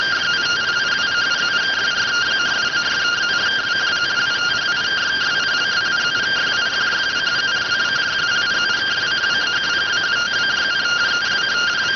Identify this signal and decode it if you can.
Two-tone teleprinter signal, about 170 Hz shift.